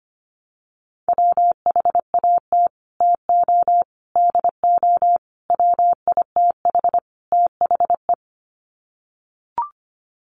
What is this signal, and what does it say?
Morse code, heard as a keyed tone, about 25 wpm, 700 Hz. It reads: WHAT TO DO WITH THE
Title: Sets of 5 Words Encore2 25wpm